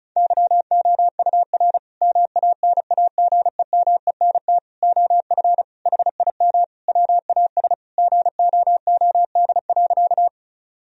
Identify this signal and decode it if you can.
Morse code, heard as a keyed tone, about 36 wpm, 700 Hz. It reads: YOUR MANAGEMENT OF HIM WAS GOOD.
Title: Sentences from Top 1000 Words - Review Part 4 - 35wpm